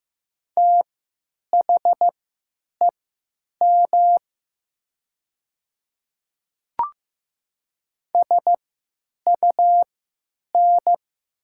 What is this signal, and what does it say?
Morse code, heard as a keyed tone, about 15 wpm, spaced as for 9 wpm, 700 Hz. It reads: THEM SUN
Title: 3x Character Spacing - Top 500 Words - 15wpm